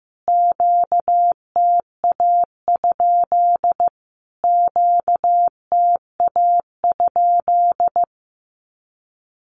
Morse code, heard as a keyed tone, about 15 wpm, 700 Hz. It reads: QTA? QTA?
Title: Q-Codes Repeated 3x 15wpm